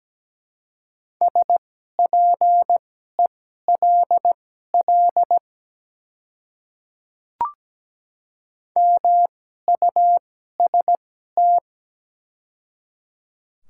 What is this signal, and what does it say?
Morse code, heard as a keyed tone, about 17 wpm, spaced as for 12 wpm, 700 Hz. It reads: SPELL MUST
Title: 2x Character Spacing - Top 200 Words - 17wpm